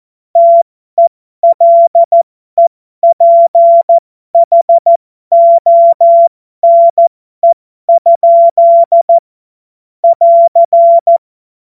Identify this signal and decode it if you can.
Morse code, heard as a keyed tone, about 14 wpm, spaced as for 12 wpm, 680 Hz. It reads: TELEPHONE? <AR>
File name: Morsekurs / Witze